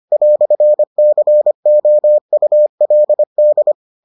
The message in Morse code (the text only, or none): "COULD